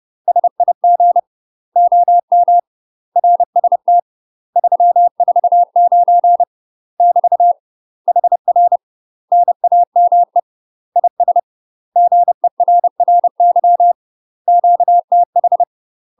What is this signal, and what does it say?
Morse code, heard as a keyed tone, about 30 wpm, 700 Hz. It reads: SIG OM RST 349 <BT> HR NAME IS GERRY QTH
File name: UK MASTER QSOs TRAINING / 30WPM / UK MASTER QSOs 0035